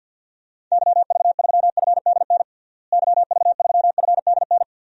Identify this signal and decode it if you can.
Morse code, heard as a keyed tone, about 47 wpm, 700 Hz. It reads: YV3FDN YV3FDN
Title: DXCC-100 Call Signs - 6 Characters - Repeated 3x - 50wpm